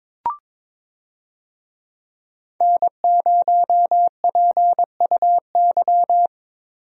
Morse code, heard as a keyed tone, about 22 wpm, 700 Hz. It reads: N0PUY
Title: USA Call Signs - 5 Characters - Repeated 2x - 22wpm